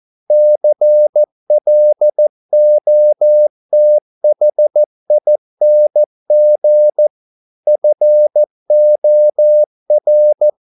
Morse code, heard as a keyed tone, about 14 wpm, 600 Hz. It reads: CLOTHING FOR